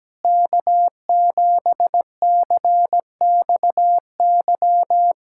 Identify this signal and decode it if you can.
Morse code, heard as a keyed tone, about 17 wpm, 700 Hz. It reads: K7CXY
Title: USA Callsign DE Callsign - 5 Characters - 17wpm